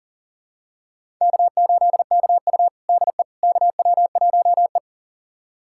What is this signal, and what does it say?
Morse code, heard as a keyed tone, about 37 wpm, 700 Hz. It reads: K8KU DE KW1E
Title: USA Callsign DE Callsign - 4 Characters - 40wpm